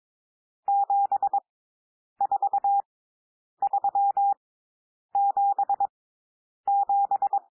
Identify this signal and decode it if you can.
Morse code, heard as a keyed tone, about 22 wpm, spaced as for 9 wpm, 800 Hz. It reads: 74377
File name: cwmp3 / N1 SOUND NUMBERS 22 wpm